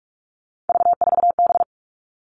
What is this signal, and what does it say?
Morse code, heard as a keyed tone, about 44 wpm, 700 Hz. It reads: V4B
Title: All Cut Numbers - 3-Digit 45wpm